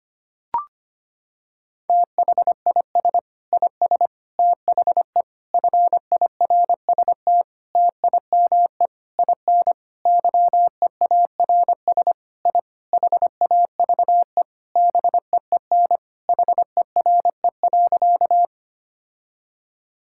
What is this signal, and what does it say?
Morse code, heard as a keyed tone, about 25 wpm, 700 Hz. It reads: THIS IS THE FIRST TIME IN YEARS I HAVE BEEN HERE.
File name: practice / 25wpm / Sentences with 4-Word Phrases - 25wpm